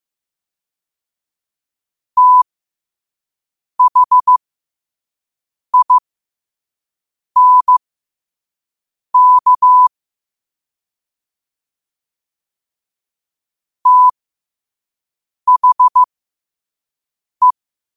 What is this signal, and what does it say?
Morse code, heard as a keyed tone, about 15 wpm, spaced as for 5 wpm, 1000 Hz. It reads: THINK THE
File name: qotdfiles / 5wpm